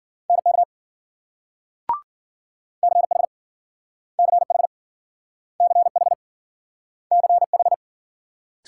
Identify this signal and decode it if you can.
Morse code, heard as a keyed tone, about 53 wpm, 700 Hz. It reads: NK CH CH CH CH